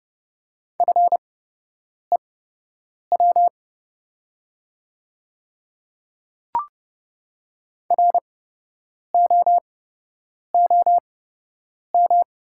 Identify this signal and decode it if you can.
Morse code, heard as a keyed tone, about 31 wpm, spaced as for 8 wpm, 700 Hz. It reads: FEW ROOM
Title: 8x Character Spacing - Top 500 Words - 30wpm